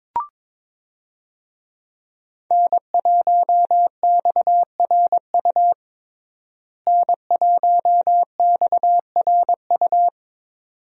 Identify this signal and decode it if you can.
Morse code, heard as a keyed tone, about 22 wpm, 700 Hz. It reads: N1XRU N1XRU
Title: USA Call Signs - 5 Characters - Repeated 2x - 22wpm